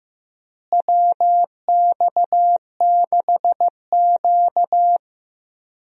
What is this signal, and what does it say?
Morse code, heard as a keyed tone, about 15 wpm, 700 Hz. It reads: WX6Q
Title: USA Call Signs - 4 Characters - Repeated 2x - 15wpm